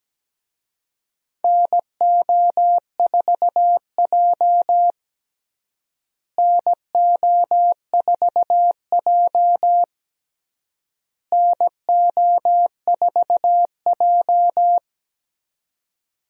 Morse code, heard as a keyed tone, about 17 wpm, 700 Hz. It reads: NO4J NO4J NO4J
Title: USA Call Signs - 4 Characters - Repeated 3x - 17wpm